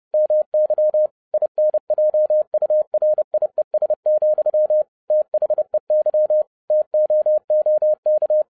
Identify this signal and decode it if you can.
Morse code, heard as a keyed tone, about 30 wpm, 610 Hz. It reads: MY INJURIES, THEY TOOK
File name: 2011-07-mp3 / n55 BACK FROM A BIG ACCIDENT 30wpm